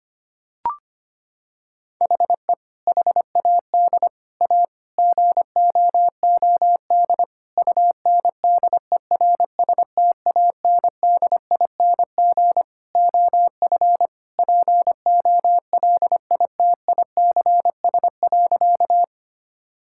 Morse code, heard as a keyed tone, about 25 wpm, 700 Hz. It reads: HE HAD A GOOD UNDERSTANDING OF POLITICS.
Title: Sentences from Top 1000 Words - Review Part 6 - 25wpm